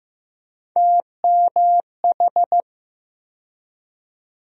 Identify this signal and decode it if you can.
Morse code, heard as a keyed tone, about 15 wpm, 700 Hz. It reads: TMH
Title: USA Call Sign Postfixes - 3 Letters 15wpm